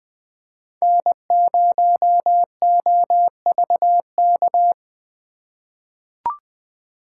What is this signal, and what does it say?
Morse code, heard as a keyed tone, about 20 wpm, 700 Hz. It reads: N0OVK
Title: USA Call Signs - 3x Character Spacing - 20wpm